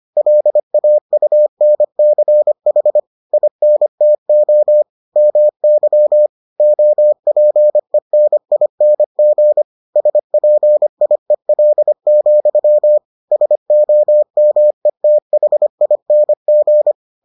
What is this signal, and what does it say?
Morse code, heard as a keyed tone, about 25 wpm, 600 Hz. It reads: LAUNCH INTO MY OPENING SPIEL, SOMETHING